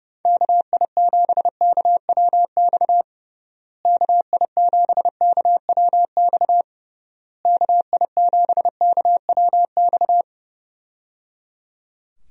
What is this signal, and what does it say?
Morse code, heard as a keyed tone, about 30 wpm, 700 Hz. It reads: KI7KWX KI7KWX KI7KWX
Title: USA Call Signs - 6 Characters - Repeated 3x - 30wpm